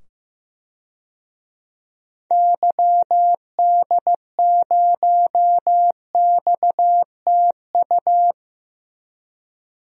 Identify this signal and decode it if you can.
Morse code, heard as a keyed tone, about 15 wpm, 700 Hz. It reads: YD0XTU
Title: DXCC-100 Call Signs - 6 Characters - 15wpm